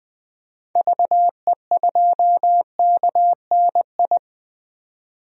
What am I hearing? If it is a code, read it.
Morse code, heard as a keyed tone, about 20 wpm, 700 Hz. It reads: VE2KNI
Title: DXCC-100 Call Signs - 6 Characters - 20wpm